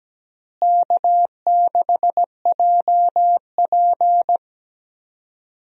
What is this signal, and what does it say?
Morse code, heard as a keyed tone, about 17 wpm, 700 Hz. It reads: K6JP
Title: USA Call Signs - 4 Characters - Repeated 3x - 17wpm